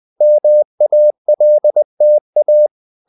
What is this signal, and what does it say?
Morse code, heard as a keyed tone, about 20 wpm, 600 Hz. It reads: MALTA
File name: CW / Practice / NationalCapitals20